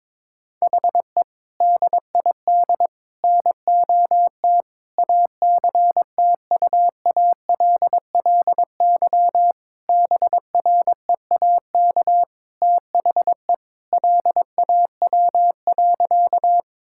Morse code, heard as a keyed tone, about 22 wpm, 700 Hz. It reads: HE DID NOT ACTUALLY BREAK THE LAW.
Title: Sentences from Top 1000 Words - Review Part 6 - 22wpm